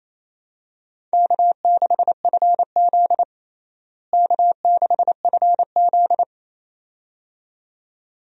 Morse code, heard as a keyed tone, about 28 wpm, 700 Hz. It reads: K6FZ K6FZ
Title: USA Call Signs - 4 Characters - Repeated 2x - 28wpm